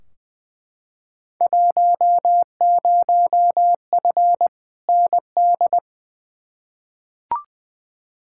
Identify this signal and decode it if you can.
Morse code, heard as a keyed tone, about 20 wpm, 700 Hz. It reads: 10F ND
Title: Field Day Exchanges Encore - 20wpm